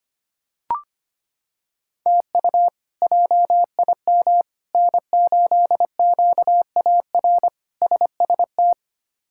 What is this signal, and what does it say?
Morse code, heard as a keyed tone, about 25 wpm, 700 Hz. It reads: TU JIM N8QAR SST
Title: K1USN SST QSO Practice - 25wpm